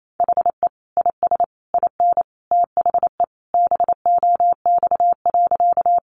Morse code, heard as a keyed tone, about 28 wpm, 700 Hz. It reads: HE IS IN THE BOX.
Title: Sentences from Top 500 Words 28wpm